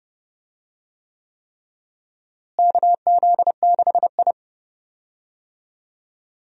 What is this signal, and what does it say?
Morse code, heard as a keyed tone, about 32 wpm, 700 Hz. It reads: KZ6I
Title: USA Call Signs - 4 Characters - Repeated 3x - 30wpm